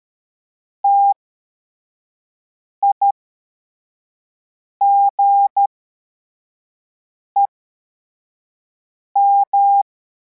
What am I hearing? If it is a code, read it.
Morse code, heard as a keyed tone, about 13 wpm, spaced as for 4 wpm, 800 Hz. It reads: TIGEM
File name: formation / cw / Text42